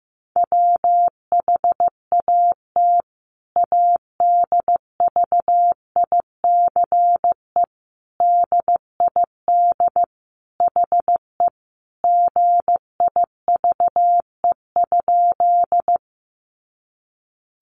Morse code, heard as a keyed tone, about 15 wpm, 700 Hz. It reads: WHAT ADVICE DID HE GIVE?
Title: Sentences from Top 1100 Words 15wpm